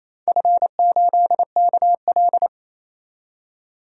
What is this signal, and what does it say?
Morse code, heard as a keyed tone, about 28 wpm, 700 Hz. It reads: F8KL
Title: DXCC-100 Call Signs - 4 Characters - 28wpm